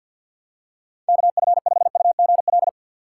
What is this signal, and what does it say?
Morse code, heard as a keyed tone, about 52 wpm, 700 Hz. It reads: KF5UGL
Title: USA Call Signs - 2x Character Spacing - 50wpm